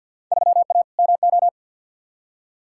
Morse code, heard as a keyed tone, about 48 wpm, 700 Hz. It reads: 3A MO
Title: Field Day Exchanges Encore - 50wpm